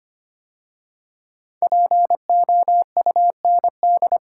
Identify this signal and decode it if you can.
Morse code, heard as a keyed tone, about 25 wpm, 700 Hz. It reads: POUND
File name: 5 Letter Words - First 3 Letters Given - 25wpm